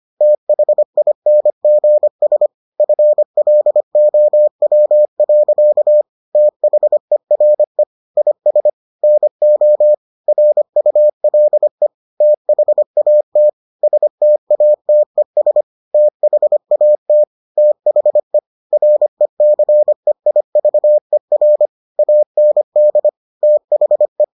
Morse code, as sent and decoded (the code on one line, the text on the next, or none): - .... .. -. --. ... / ..-. .-.. --- .-- .-.-.- / - .... .-. / .. ... / -. --- / .-. ..- .-.. / - .... .- - / ... - .- - . ... / - .... .- - / - .... / .-. -.-. .. ...- . .-. / .- -. -.. / - .... .
THINGS FLOW. THERE IS NO RULE THAT STATES THAT THE RECEIVER AND THE